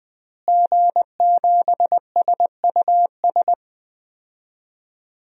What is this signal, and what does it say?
Morse code, heard as a keyed tone, about 20 wpm, 700 Hz. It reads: G7SUS
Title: DXCC-100 Call Signs - 5 Characters - Repeated 3x - 20wpm